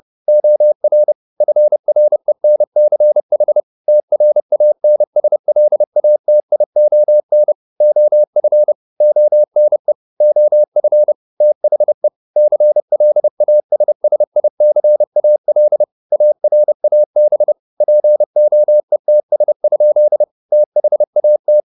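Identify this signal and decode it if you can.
Morse code, heard as a keyed tone, about 30 wpm, 600 Hz. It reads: OR FRENCH TRANSLATION OF ONE OF THE CLASSICAL ARAB POETS? THAT
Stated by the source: Morse code at 30 wpm, the novel Never by Ken Follett in English, from CW med Gnister.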